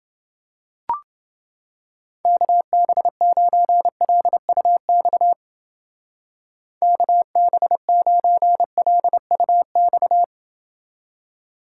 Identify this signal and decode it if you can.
Morse code, heard as a keyed tone, about 29 wpm, 700 Hz. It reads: KB9LUX KB9LUX